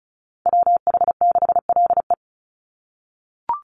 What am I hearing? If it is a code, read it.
Morse code, heard as a keyed tone, about 34 wpm, 700 Hz. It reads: WH6LE
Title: NA SOTA Chaser Callsigns 35wpm